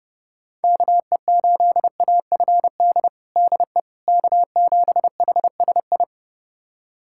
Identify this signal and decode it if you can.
Morse code, heard as a keyed tone, about 30 wpm, 700 Hz. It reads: KE8AFD DE K7HSI
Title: USA Callsign DE Callsign 30wpm